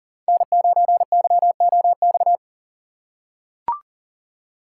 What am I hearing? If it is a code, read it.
Morse code, heard as a keyed tone, about 40 wpm, 700 Hz. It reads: N9YOX
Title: USA Call Signs - 5 Characters - 2x Character Spacing - 40wpm